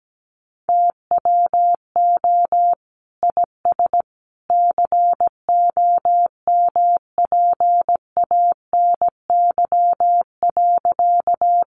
Morse code, heard as a keyed tone, about 17 wpm, 700 Hz. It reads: TWO IS COMPANY.
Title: Sentences from Top 600 Words 17wpm